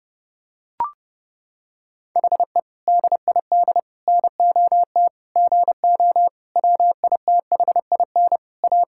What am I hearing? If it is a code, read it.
Morse code, heard as a keyed tone, about 30 wpm, 700 Hz. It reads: HE DID NOT GO WITHIN A
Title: Sentences from Top 1600 Words 30wpm